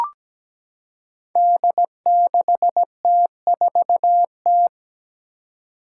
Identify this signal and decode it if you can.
Morse code, heard as a keyed tone, about 17 wpm, 700 Hz. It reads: D6T4T
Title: Five Challenging Letter-Number Combinations 17wpm